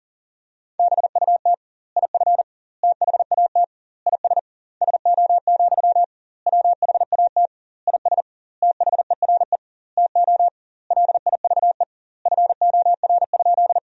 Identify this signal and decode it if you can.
Morse code, heard as a keyed tone, about 40 wpm, 700 Hz. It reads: BUT IF THAT IS SO, WHAT IS THERE TO LIVE FOR?
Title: 2x Word-Spacing - Sentences from Top 200 Words - 40wpm